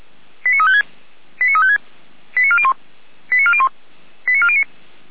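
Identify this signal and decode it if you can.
Five-tone selective call (ZVEI), heard as quick runs of five tones, tones of about 70 ms. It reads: 89356 89356 89401 89401 89409
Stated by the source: ZVEI five-tone selective call received by SDR, from audio.f85360.feuerwehr-orlamuende.org.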